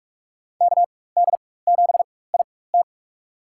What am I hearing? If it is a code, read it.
Morse code, heard as a keyed tone, about 47 wpm, spaced as for 21 wpm, 690 Hz. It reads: KD7IT
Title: USA Call Signs - 5 Characters - 4x Character Spacing - 45wpm